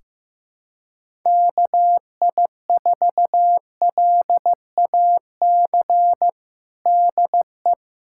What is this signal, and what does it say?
Morse code, heard as a keyed tone, about 15 wpm, 700 Hz. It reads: KI4LAC DE
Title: USA Callsign DE Callsign - 6 Characters - 15wpm